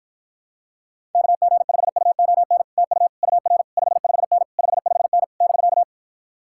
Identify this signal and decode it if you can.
Morse code, heard as a keyed tone, about 46 wpm, 700 Hz. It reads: KG5UON TU UR 55N 55N <BK>